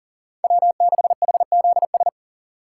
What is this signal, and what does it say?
Morse code, heard as a keyed tone, about 40 wpm, 690 Hz. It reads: W6HZS
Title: USA Call Signs - 5 Characters - 3x Character Spacing - 40wpm